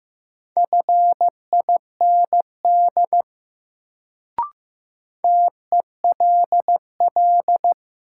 Morse code, heard as a keyed tone, about 15 wpm, 700 Hz. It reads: FIND TELL
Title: Sets of 2 Words - New Word Drill - 15wpm